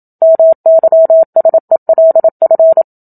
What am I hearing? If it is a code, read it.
Morse code, heard as a keyed tone, about 27 wpm, 650 Hz. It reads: MYSELF
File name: Book Two / 01-Chapter 1 - 27 WPM